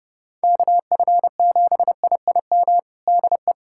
Morse code, heard as a keyed tone, about 30 wpm, 700 Hz. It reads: KF7IIM DE
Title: USA Callsign DE Callsign - 6 Characters - 30wpm